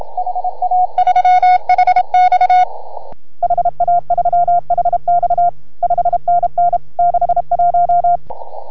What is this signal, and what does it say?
Morse code, heard as a keyed tone, about 33 wpm, 690 Hz. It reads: HA3HX HA3HX 5NN 61